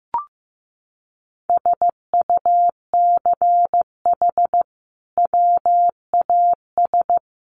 Morse code, heard as a keyed tone, about 15 wpm, 700 Hz. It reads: SUCH WAS
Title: Sentences from Top 1200 Words 15wpm